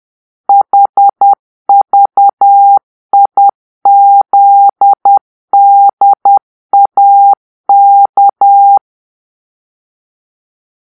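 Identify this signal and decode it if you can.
Morse code, heard as a keyed tone, about 10 wpm, 800 Hz. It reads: HVIZDAK
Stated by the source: Morse-code cipher made by a player team for the Kachna puzzle game.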